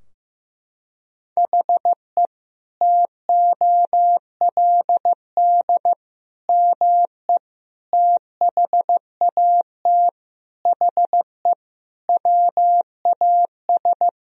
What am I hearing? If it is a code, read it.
Morse code, heard as a keyed tone, about 15 wpm, 700 Hz. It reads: HE TOLD ME THAT HE WAS
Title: Sentences from Top 1000 Words - Review Part 3 - 15wpm